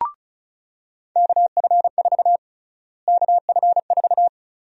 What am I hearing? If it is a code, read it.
Morse code, heard as a keyed tone, about 34 wpm, 700 Hz. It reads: KF4 KF4
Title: USA Call Sign Prefixes - Repeated 2x - 35wpm